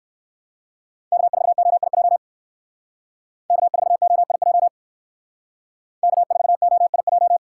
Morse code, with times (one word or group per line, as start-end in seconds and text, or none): 1.1-2.2 K4OIJ
3.5-4.7 K4OIJ
6.0-7.4 K4OIJ